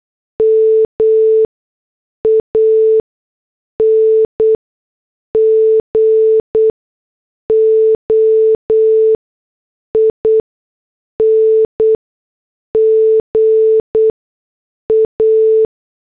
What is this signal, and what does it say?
Morse code, heard as a keyed tone, about 8 wpm, spaced as for 6 wpm, 440 Hz. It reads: MAN GOING A
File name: morse / headlines